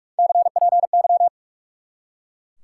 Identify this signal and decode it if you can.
Morse code, heard as a keyed tone, about 43 wpm, 700 Hz. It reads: CPY
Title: Top 90 QSO Elements 45wpm